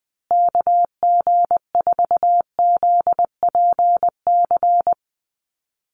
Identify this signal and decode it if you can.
Morse code, heard as a keyed tone, about 20 wpm, 700 Hz. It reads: KG4ZPC
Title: USA Call Signs Repeated 2x 20wpm